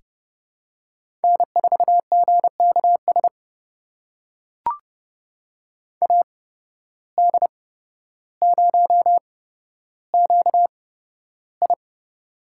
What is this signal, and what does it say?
Morse code, heard as a keyed tone, about 30 wpm, spaced as for 13 wpm, 700 Hz. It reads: N4GKS AD0QI
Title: USA Call Signs - 5 Characters - 8x Character Spacing - 30wpm